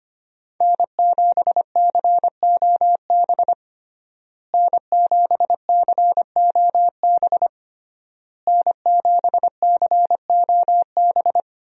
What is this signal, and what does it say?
Morse code, heard as a keyed tone, about 25 wpm, 700 Hz. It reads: N7COB N7COB N7COB
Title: USA Call Signs - 5 Characters - Repeated 3x - 25wpm